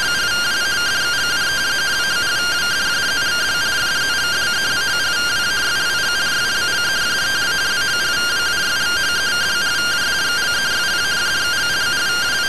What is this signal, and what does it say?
Two-tone teleprinter signal, about 170 Hz shift.